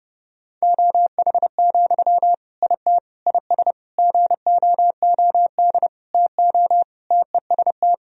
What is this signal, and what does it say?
Morse code, heard as a keyed tone, about 30 wpm, 700 Hz. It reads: OH, IT IS GOOD TO TEST
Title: Sentences from Top 1000 Words - Review Part 3 - 30wpm